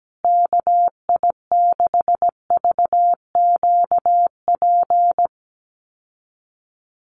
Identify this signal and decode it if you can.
Morse code, heard as a keyed tone, about 17 wpm, 700 Hz. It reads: KI6VQP
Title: USA Call Signs Repeated 3x 17wpm